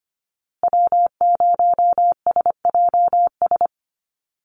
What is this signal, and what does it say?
Morse code, heard as a keyed tone, about 25 wpm, 700 Hz. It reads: W0SJS
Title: Call Signs Repeated 2x - 25wpm